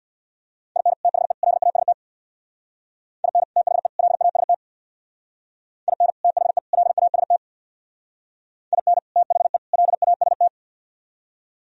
Morse code, heard as a keyed tone, about 64 wpm, 700 Hz. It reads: IN THE LAST IN THE LAST IN THE LAST IN THE LAST